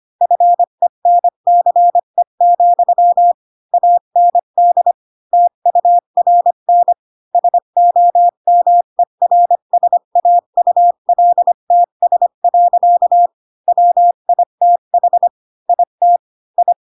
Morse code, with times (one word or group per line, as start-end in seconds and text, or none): 0.2-3.3 FENCE,
3.7-4.9 AND
5.3-6.9 TURN
7.3-13.3 SOMERSAULTS.
13.7-15.3 WITH
15.7-16.2 IT
16.6-16.7 I